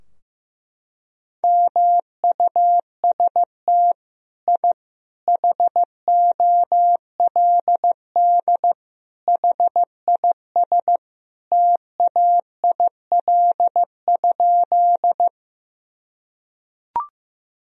Morse code, heard as a keyed tone, about 15 wpm, 700 Hz. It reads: MUST I HOLD HIS TAIL?
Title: Sentences from Top 1000 Words - Review Part 3 - 15wpm